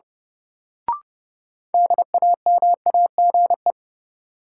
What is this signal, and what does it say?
Morse code, heard as a keyed tone, about 32 wpm, 700 Hz. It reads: DAMAGE
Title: Sentences from Top 1100 Words - New Word Drill - 30wpm